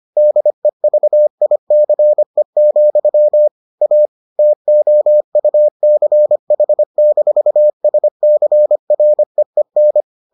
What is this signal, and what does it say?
Morse code, heard as a keyed tone, about 25 wpm, 600 Hz. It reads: DEVICE, A TOUCH-SCREEN